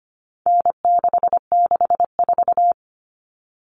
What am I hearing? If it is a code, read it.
Morse code, heard as a keyed tone, about 25 wpm, 700 Hz. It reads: N664
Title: All Cut Numbers - 4-Digit 25wpm